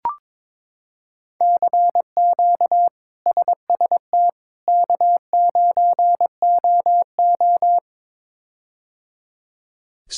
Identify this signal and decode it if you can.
Morse code, heard as a keyed tone, about 22 wpm, 700 Hz. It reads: CQ SST K9OO
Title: K1USN SST QSO Practice - 22wpm